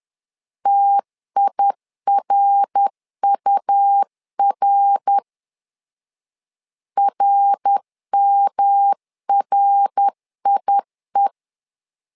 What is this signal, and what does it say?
Morse code, heard as a keyed tone, about 10 wpm, 800 Hz. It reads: TIRUR RMRIE
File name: LEZIONI / 3 MWRO / 3 MWR 10wpm 1952